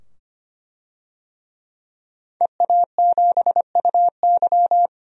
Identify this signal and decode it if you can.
Morse code, heard as a keyed tone, about 25 wpm, 700 Hz. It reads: EA7UY
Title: DXCC-100 Call Signs - 5 Characters - 25wpm